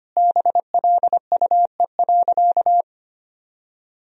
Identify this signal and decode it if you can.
Morse code, heard as a keyed tone, about 25 wpm, 700 Hz. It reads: BLUE.
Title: Sentences from Top 1000 Words - Review Part 3 - 25wpm